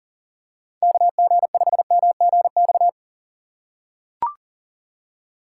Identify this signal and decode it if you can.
Morse code, heard as a keyed tone, about 40 wpm, 700 Hz. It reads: KG5MGX
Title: USA Call Signs - 6 Characters - 2x Character Spacing - 40wpm